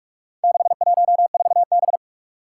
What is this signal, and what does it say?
Morse code, heard as a keyed tone, about 46 wpm, 700 Hz. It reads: 614B
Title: Four Challenging Letter-Number Combinations 45wpm